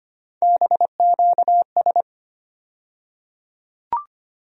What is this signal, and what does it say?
Morse code, heard as a keyed tone, about 25 wpm, 700 Hz. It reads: BQS
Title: USA Call Sign Postfixes - 3 Letters 25wpm